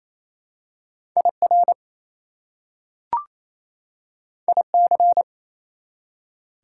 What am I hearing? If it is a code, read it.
Morse code, heard as a keyed tone, about 28 wpm, 700 Hz. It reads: IR IC
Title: Prefixes and Suffixes 28wpm